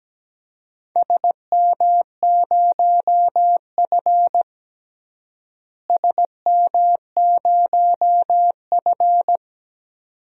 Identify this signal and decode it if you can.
Morse code, heard as a keyed tone, about 17 wpm, 700 Hz. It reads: SM0F SM0F
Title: DXCC-100 Call Signs - 4 Characters - Repeated 2x - 17wpm